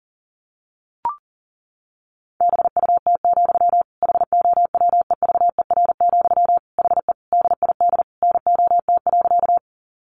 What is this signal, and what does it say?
Morse code, heard as a keyed tone, about 40 wpm, 700 Hz. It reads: BUT, HOWEVER, HE DID NOT.
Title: Sentences from Top 600 Words 40wpm